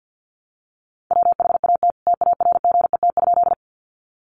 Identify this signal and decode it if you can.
Morse code, heard as a keyed tone, about 50 wpm, 700 Hz. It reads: WHAT TARGET?
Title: Sentences from Top 1200 Words 50wpm